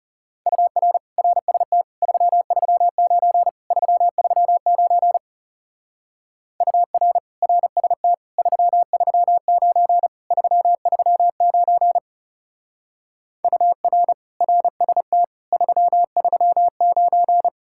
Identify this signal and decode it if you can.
Morse code, heard as a keyed tone, about 40 wpm, 700 Hz. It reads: UR RST 339 339 UR RST 339 339 UR RST 339